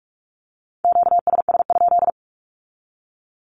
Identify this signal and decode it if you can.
Morse code, heard as a keyed tone, about 46 wpm, 700 Hz. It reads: QSS?
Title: Q-Codes with Spoken Meaning - 45wpm